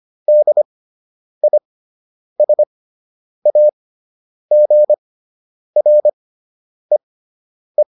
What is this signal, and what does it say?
Morse code, heard as a keyed tone, about 25 wpm, spaced as for 9 wpm, 600 Hz. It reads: DISAGREE